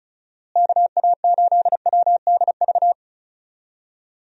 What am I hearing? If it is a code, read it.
Morse code, heard as a keyed tone, about 34 wpm, 700 Hz. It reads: KA8WDV
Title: USA Call Signs - 6 Characters - Repeated 3x - 35wpm